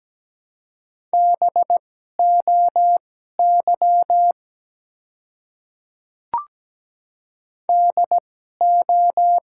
Morse code, heard as a keyed tone, about 17 wpm, spaced as for 12 wpm, 700 Hz. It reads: BOY DO
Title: 2x Character Spacing - Top 200 Words - 17wpm